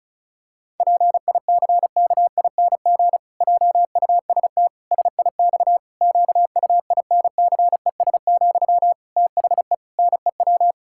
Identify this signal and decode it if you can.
Morse code, heard as a keyed tone, about 36 wpm, 700 Hz. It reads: PICKING JUST SIX QUINCES, THE NEW
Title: Pangram Challenge v2 - 35wpm